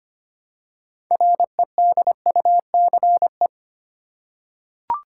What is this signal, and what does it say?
Morse code, heard as a keyed tone, about 25 wpm, 700 Hz. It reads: REDUCE
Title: Six Letter Words - 25wpm